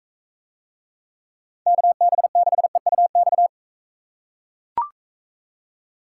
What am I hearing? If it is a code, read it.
Morse code, heard as a keyed tone, about 41 wpm, 700 Hz. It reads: KB6EUX